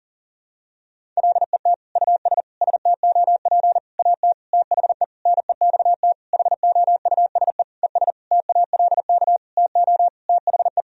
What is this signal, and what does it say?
Morse code, heard as a keyed tone, about 40 wpm, 700 Hz. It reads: LET US STOP AT THE NEXT HOUSE ES TALK TO THE
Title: Sentences from Top 400 Words and Abbreviations 40wpm